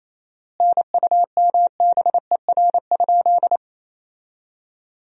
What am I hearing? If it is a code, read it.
Morse code, heard as a keyed tone, about 27 wpm, 700 Hz. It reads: NUMBER?
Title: 4x Word-Spacing - Sentences from Top 100 Words - 28wpm